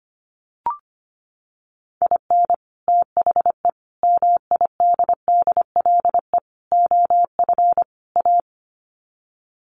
Morse code, heard as a keyed tone, about 25 wpm, 700 Hz. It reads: IN THE MIDDLE OF A
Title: Sets of 5 Words 25wpm